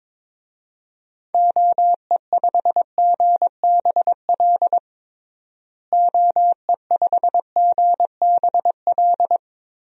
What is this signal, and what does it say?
Morse code, heard as a keyed tone, about 22 wpm, 700 Hz. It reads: OE5GBL OE5GBL
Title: DXCC-100 Call Signs - 6 Characters - Repeated 2x - 22wpm